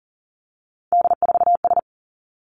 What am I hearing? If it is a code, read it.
Morse code, heard as a keyed tone, about 40 wpm, 690 Hz. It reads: D4S
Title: Three Challenging Letter-Number Combinations 40wpm